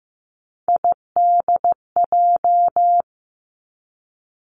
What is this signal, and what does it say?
Morse code, heard as a keyed tone, about 15 wpm, 700 Hz. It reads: IDJ
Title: Three Challenging Letter-Number Combinations 15wpm